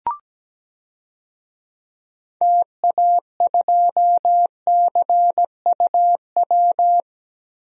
Morse code, heard as a keyed tone, about 17 wpm, 700 Hz. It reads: TA2CUW
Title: DXCC-100 Call Signs - 6 Characters - Repeated 2x - 17wpm